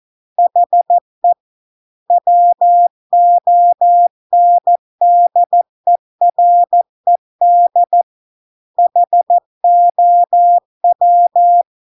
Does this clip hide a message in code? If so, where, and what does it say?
Yes: Morse code from 0.4 to 11.6 s, heard as a keyed tone, about 14 wpm, 700 Hz. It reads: HE WONDERED HOW